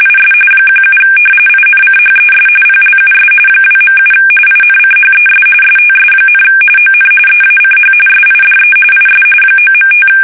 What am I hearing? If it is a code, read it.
RTTY, about 45.45 baud. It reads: IS 17 INCHES PLUS 2.7 INCHES STS, ON EACH SIDE IS 9.5
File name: fsk850 / battleshipnj